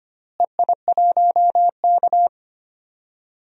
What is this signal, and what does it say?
Morse code, heard as a keyed tone, about 25 wpm, 700 Hz. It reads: EI1K
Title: DXCC-100 Call Signs - 4 Characters - Repeated 2x - 25wpm